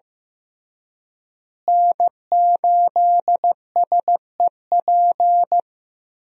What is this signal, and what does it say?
Morse code, heard as a keyed tone, about 15 wpm, 700 Hz. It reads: N8SEP
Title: USA Call Signs - 5 Characters - Repeated 3x - 15wpm